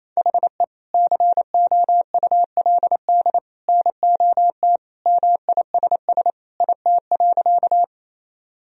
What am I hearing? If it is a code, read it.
Morse code, heard as a keyed tone, about 27 wpm, 700 Hz. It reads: HE COULD NOT MISS IT.
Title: Sentences from Top 1000 Words - Review Part 3 - 28wpm